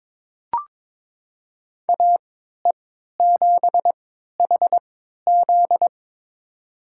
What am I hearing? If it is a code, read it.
Morse code, heard as a keyed tone, about 22 wpm, spaced as for 12 wpm, 700 Hz. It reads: AE7HZ
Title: USA Call Signs - 5 Characters - 3x Character Spacing - 22wpm